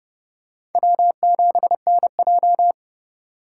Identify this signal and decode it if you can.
Morse code, heard as a keyed tone, about 30 wpm, 700 Hz. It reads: W7NJ
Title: USA Call Signs - 3x Character Spacing - 30wpm